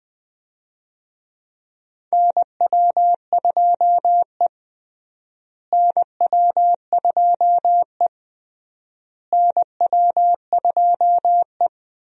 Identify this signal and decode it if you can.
Morse code, heard as a keyed tone, about 20 wpm, 700 Hz. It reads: NW2E NW2E NW2E
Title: USA Call Signs - 4 Characters - Repeated 3x - 20wpm